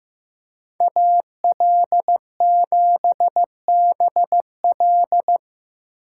Morse code, heard as a keyed tone, about 15 wpm, 700 Hz. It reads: AL7BL
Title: USA Call Signs - 5 Characters - 3x Character Spacing - 15wpm